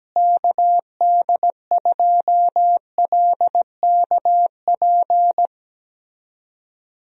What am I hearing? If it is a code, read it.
Morse code, heard as a keyed tone, about 17 wpm, 700 Hz. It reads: KD2LKP
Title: USA Call Signs - 6 Characters - Repeated 3x - 17wpm